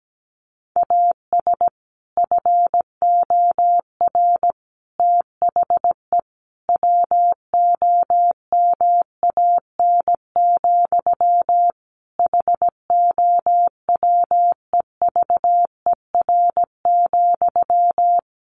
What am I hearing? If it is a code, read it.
Morse code, heard as a keyed tone, about 17 wpm, 700 Hz. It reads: AS FOR THE WOMAN, HOWEVER,
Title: Sentences from Top 1100 Words 17wpm